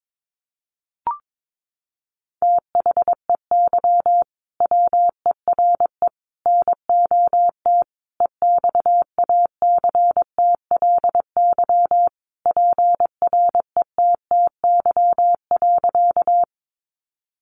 Morse code, heard as a keyed tone, about 22 wpm, 700 Hz. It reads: THEY WERE NOT EXACTLY PRETTY.
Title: Sentences from Top 1000 Words 22wpm